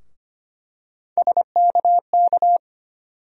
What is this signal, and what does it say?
Morse code, heard as a keyed tone, about 25 wpm, 700 Hz. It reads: SKK
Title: USA Call Sign Postfixes - 3 Letters 25wpm